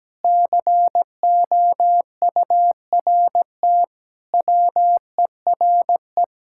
Morse code, heard as a keyed tone, about 17 wpm, 700 Hz. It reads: COURT WERE
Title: Sentences from Top 1600 Words 17wpm